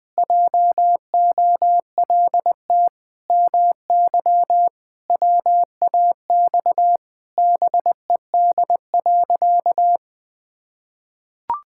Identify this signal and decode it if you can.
Morse code, heard as a keyed tone, about 20 wpm, 700 Hz. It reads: JOLT MY WAX BED.
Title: Pangram Challenge v2 - 20wpm